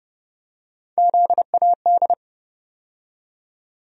Morse code, heard as a keyed tone, about 30 wpm, 700 Hz. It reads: ZAD
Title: USA Call Sign Postfixes - 3 Letters 30wpm